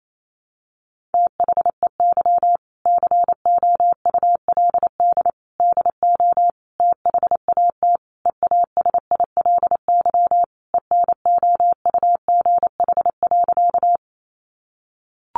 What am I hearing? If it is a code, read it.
Morse code, heard as a keyed tone, about 28 wpm, 700 Hz. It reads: THEY COULD DO THAT EASILY ENOUGH.
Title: Sentences from Top 900 Words 28wpm